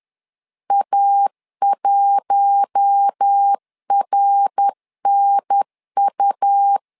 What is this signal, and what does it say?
Morse code, heard as a keyed tone, about 10 wpm, 800 Hz. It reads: A1RNU
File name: LEZIONI / 4 Q1 10wpm 2448